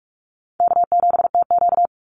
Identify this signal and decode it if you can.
Morse code, heard as a keyed tone, about 44 wpm, 700 Hz. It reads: K7TQ
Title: NA SOTA Chaser Callsigns 45wpm